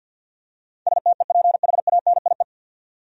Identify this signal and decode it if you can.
Morse code, heard as a keyed tone, about 48 wpm, 700 Hz. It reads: STEPHANIE
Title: USA Common Names - 50wpm